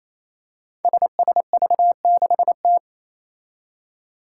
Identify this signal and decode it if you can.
Morse code, heard as a keyed tone, about 28 wpm, 700 Hz. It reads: SSV6T
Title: Five Challenging Letter-Number Combinations 28wpm